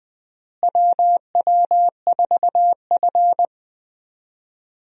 Morse code, heard as a keyed tone, about 20 wpm, 700 Hz. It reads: WW4F
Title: USA Call Signs - 4 Characters - Repeated 3x - 20wpm